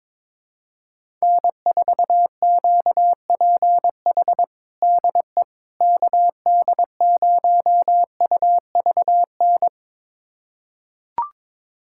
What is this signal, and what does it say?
Morse code, heard as a keyed tone, about 21 wpm, 700 Hz. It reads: N4QPH DE KD0UVN
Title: USA Callsign DE Callsign 22wpm